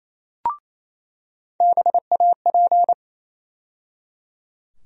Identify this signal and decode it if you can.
Morse code, heard as a keyed tone, about 28 wpm, 700 Hz. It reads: BAP
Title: USA Call Sign Postfixes - 3 Letters 28wpm